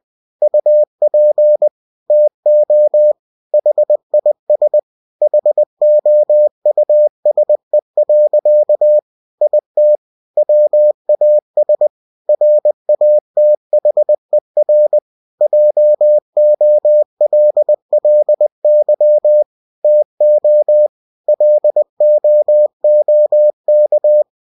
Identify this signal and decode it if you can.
Morse code, heard as a keyed tone, about 20 wpm, 600 Hz. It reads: UP TO HIS HOUSE. IT WAS RATHER JOLLY TO LOOK
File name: cw-books / winnie-the-pooh / winnie-20wpm-page-0095